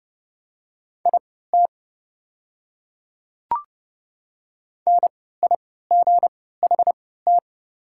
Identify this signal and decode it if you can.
Morse code, heard as a keyed tone, about 30 wpm, spaced as for 17 wpm, 690 Hz. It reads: IT NIGHT
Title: 3x Character Spacing - Top 300 Words - 30wpm